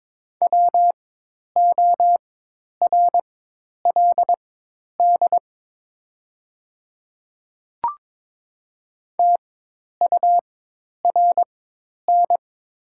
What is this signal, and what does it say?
Morse code, heard as a keyed tone, about 22 wpm, spaced as for 10 wpm, 690 Hz. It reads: WORLD TURN
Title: 4x Character Spacing - Top 300 Words - 22wpm